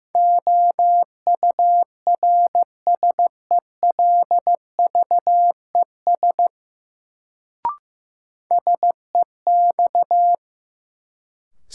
Morse code, heard as a keyed tone, about 15 wpm, 700 Hz. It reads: OURSELVES SEX
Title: Sentences from Top 1600 Words - New Word Drill Alt - 15wpm